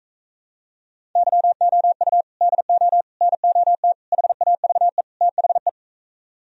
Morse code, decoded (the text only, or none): YOU DO NOT HAVE THE